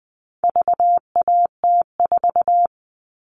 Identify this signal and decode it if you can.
Morse code, heard as a keyed tone, about 20 wpm, 700 Hz. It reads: VAT4
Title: All Cut Numbers - 4-Digit 20wpm